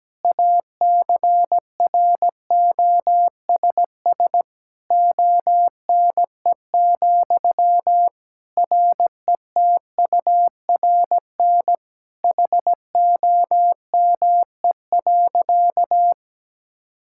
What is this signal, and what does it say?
Morse code, heard as a keyed tone, about 17 wpm, 700 Hz. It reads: ACROSS ONE, RETURN HOME.
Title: Sentences from Top 1000 Words - Review Part 5 - 17wpm